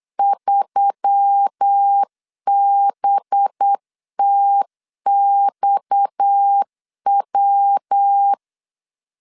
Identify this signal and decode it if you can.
Morse code, heard as a keyed tone, about 8 wpm, 800 Hz. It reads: 3BTXW
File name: LEZIONI / 7 GF9Z / 7 GF9 8wpm 3545